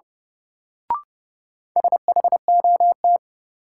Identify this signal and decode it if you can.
Morse code, heard as a keyed tone, about 32 wpm, 700 Hz. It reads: SHOT
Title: Sentences from Top 1100 Words - New Word Drill Alt - 30wpm